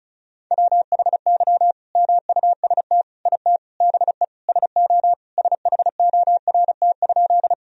Morse code, heard as a keyed tone, about 36 wpm, 700 Hz. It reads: WHY MUST IT BE SO SHORT?
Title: Sentences from Top 1000 Words - Review Part 2 - 35wpm